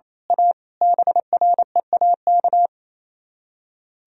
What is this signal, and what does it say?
Morse code, heard as a keyed tone, about 26 wpm, 700 Hz. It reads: A BREAK
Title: Speed-Racing - Sets of 5 Words 28wpm